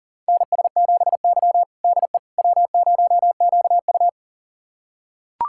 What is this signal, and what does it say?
Morse code, heard as a keyed tone, about 40 wpm, 700 Hz. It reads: NS7Y DE W0QU
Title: USA Callsign DE Callsign - 4 Characters - 40wpm